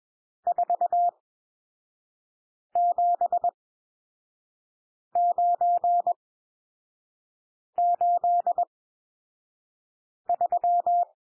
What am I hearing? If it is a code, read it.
Morse code, heard as a keyed tone, about 21 wpm, spaced as for 5 wpm, 700 Hz. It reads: 47983
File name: cwmp3 / N1 SOUND NUMBERS 21 wpm